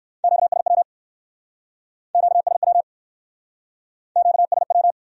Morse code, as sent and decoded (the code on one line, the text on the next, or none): --.- ... .-- / --.- ... .-- / --.- ... .--
QSW QSW QSW